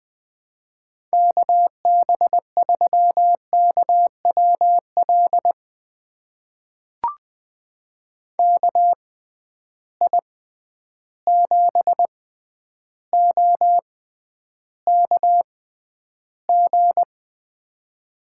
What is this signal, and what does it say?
Morse code, heard as a keyed tone, about 20 wpm, spaced as for 10 wpm, 700 Hz. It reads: KB3KWL KI7OKG
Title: USA Call Signs - 6 Characters - 6x Character Spacing - 20wpm